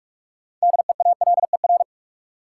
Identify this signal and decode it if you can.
Morse code, heard as a keyed tone, about 46 wpm, 700 Hz. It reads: DEALER
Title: Sentences from Top 1100 Words - New Word Drill - 45wpm